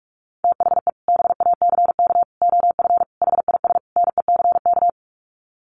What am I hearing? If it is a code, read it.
Morse code, heard as a keyed tone, about 44 wpm, 700 Hz. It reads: THE BACK OF HIS NECK
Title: Sets of 5 Words Encore 45wpm